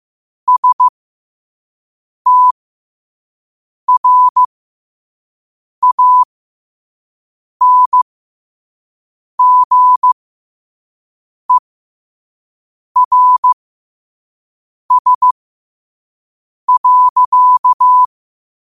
Morse code, heard as a keyed tone, about 15 wpm, spaced as for 5 wpm, 1000 Hz. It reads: STRANGERS.
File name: qotdfiles / 5wpm